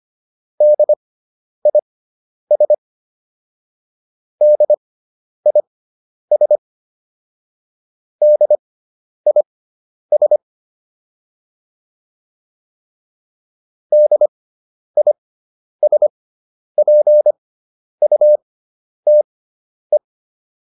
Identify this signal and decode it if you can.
Morse code, heard as a keyed tone, about 25 wpm, spaced as for 10 wpm, 600 Hz. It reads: DIS DIS DIS DISPUTE